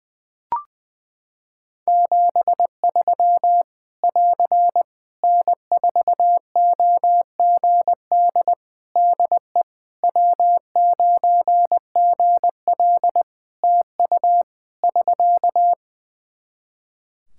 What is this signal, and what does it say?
Morse code, heard as a keyed tone, about 20 wpm, 700 Hz. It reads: 73 <AR> N4OGD DE W9GL TU <SK>
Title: K1USN SST QSO Practice 20wpm